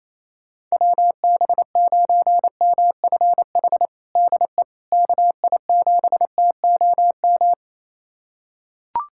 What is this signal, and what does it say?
Morse code, heard as a keyed tone, about 27 wpm, 700 Hz. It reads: WB9MFH DE KI7TOM
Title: USA Callsign DE Callsign - 6 Characters - 28wpm